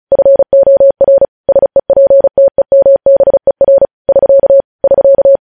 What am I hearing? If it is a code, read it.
Morse code, heard as a keyed tone, about 36 wpm, 550 Hz. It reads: FOR SEPTEMBER <SK> <SK>
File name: podcast / News headlines-1x-35x35